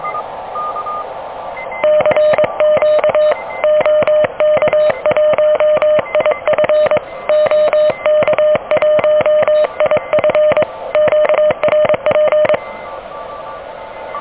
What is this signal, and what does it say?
Morse code, heard as a keyed tone, about 23 wpm, 610 Hz. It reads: CQ OK1IF OK1IF QRP
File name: OK QRP 2023 / OK QRP 2023